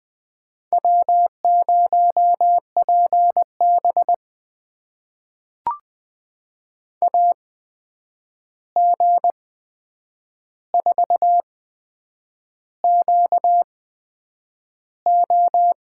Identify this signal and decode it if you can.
Morse code, heard as a keyed tone, about 20 wpm, spaced as for 8 wpm, 700 Hz. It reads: W0PB AG4QO
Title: USA Call Signs - 8x Character Spacing - 20wpm